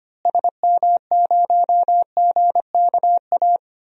Morse code, heard as a keyed tone, about 25 wpm, 700 Hz. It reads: SM0GKA
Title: DXCC-100 Call Signs - 6 Characters - Repeated 3x - 25wpm